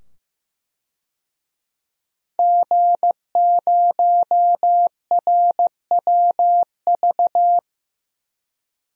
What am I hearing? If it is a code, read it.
Morse code, heard as a keyed tone, about 15 wpm, 700 Hz. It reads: G0RWV
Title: DXCC-100 Call Signs - 5 Characters - 15wpm